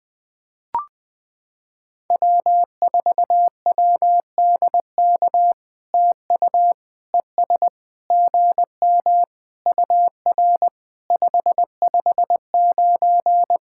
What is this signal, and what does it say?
Morse code, heard as a keyed tone, about 20 wpm, 700 Hz. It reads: W4WDK TU ES GM UR 559
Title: POTA QSO Practice - 20wpm